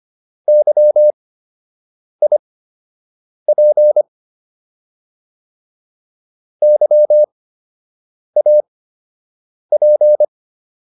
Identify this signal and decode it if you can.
Morse code, heard as a keyed tone, about 25 wpm, spaced as for 7 wpm, 600 Hz. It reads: YIP YAP